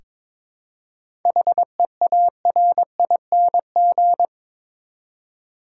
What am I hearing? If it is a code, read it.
Morse code, heard as a keyed tone, about 22 wpm, 700 Hz. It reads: HEARING
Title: Seven Letter Words - 22wpm